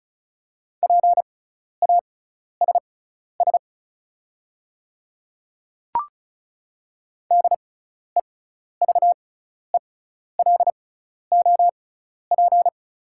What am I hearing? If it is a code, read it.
Morse code, heard as a keyed tone, about 36 wpm, spaced as for 12 wpm, 690 Hz. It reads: PASS DEVELOP
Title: 6x Character Spacing - Top 500 Words - 35wpm